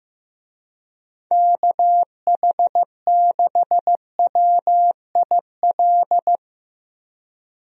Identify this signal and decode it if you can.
Morse code, heard as a keyed tone, about 15 wpm, 700 Hz. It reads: KH6WIL
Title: USA Call Signs - 6 Characters - Repeated 3x - 15wpm